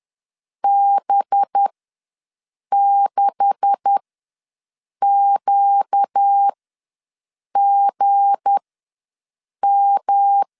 Morse code, heard as a keyed tone, about 10 wpm, spaced as for 6 wpm, 800 Hz. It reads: B6QGM
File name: LEZIONI / 9 286 ws 10wpm 4032